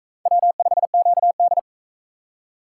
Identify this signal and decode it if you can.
Morse code, heard as a keyed tone, about 37 wpm, 700 Hz. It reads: W5QD